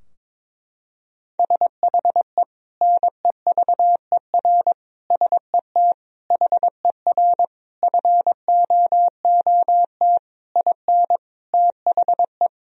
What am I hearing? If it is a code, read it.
Morse code, heard as a keyed tone, about 22 wpm, 700 Hz. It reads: SHE NEVER SET HER FOOT IN THE
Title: Sentences from Top 1000 Words - Review Part 3 - 22wpm